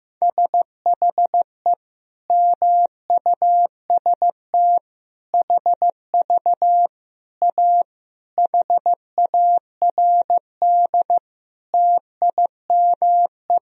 Morse code, heard as a keyed tone, about 16 wpm, 700 Hz. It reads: SHE MUST HV A HARD TIME
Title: Sentences from Top 300 Words and Abbreviations 15wpm